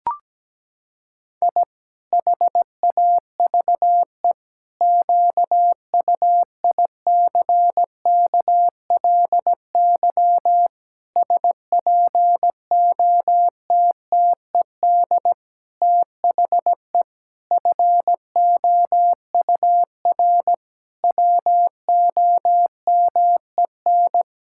Morse code, heard as a keyed tone, about 17 wpm, 700 Hz. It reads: I HAVE QUICKLY SPOTTED THE FOUR WOMEN
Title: Pangram Challenge v2 - 17wpm